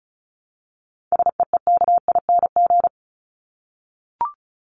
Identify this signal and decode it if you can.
Morse code, heard as a keyed tone, about 34 wpm, 700 Hz. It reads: SEEKING
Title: ING Suffix - Top 100 Words with Base Words 35wpm